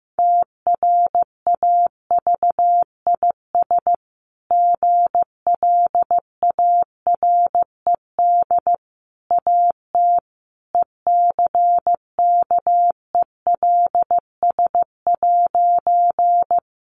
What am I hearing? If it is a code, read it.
Morse code, heard as a keyed tone, about 15 wpm, 700 Hz. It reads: TRAVIS GLARED AT ECKELS'
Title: A Sound Of Thunder 15wpm